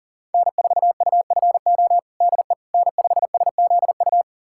Morse code, heard as a keyed tone, about 40 wpm, 700 Hz. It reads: N4UFO DE N5SZU
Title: USA Callsign DE Callsign - 5 Characters - 40wpm